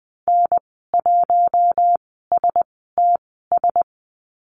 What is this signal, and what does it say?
Morse code, heard as a keyed tone, about 20 wpm, spaced as for 14 wpm, 700 Hz. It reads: N1STS
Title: 082 Lesson 14g - Number 2 - Call Signs 20wpm